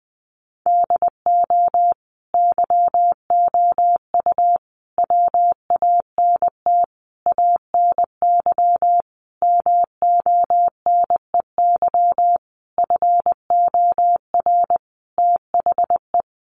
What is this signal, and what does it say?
Morse code, heard as a keyed tone, about 20 wpm, 700 Hz. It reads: DO YOU WANT ANY MONEY FOR THE
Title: Sentences from Top 1400 Words 20wpm